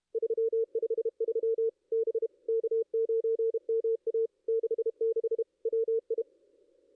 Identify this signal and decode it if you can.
Morse code, heard as a keyed tone, about 31 wpm, 440 Hz. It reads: 353 B K9MA 66 WI
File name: ss-cw-k5tr-op-n5rz